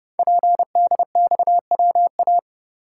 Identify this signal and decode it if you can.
Morse code, heard as a keyed tone, about 30 wpm, 700 Hz. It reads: PDXWA
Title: Code Groups - 5 Letter:Numbers - 30wpm